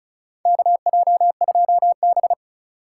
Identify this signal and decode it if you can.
Morse code, heard as a keyed tone, about 35 wpm, 690 Hz. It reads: KJ2B
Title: USA Call Signs - 4 Characters - 2x Character Spacing - 35wpm